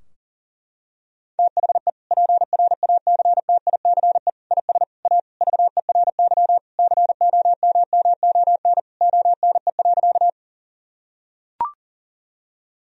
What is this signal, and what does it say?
Morse code, heard as a keyed tone, about 40 wpm, 700 Hz. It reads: THE PRACTICE IS A VERY COMMON ONE.
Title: Sentences from Top 1000 Words - Review Part 4 - 40wpm